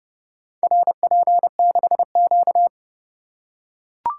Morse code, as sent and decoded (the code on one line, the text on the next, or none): .-. .--. -.... --.-
RP6Q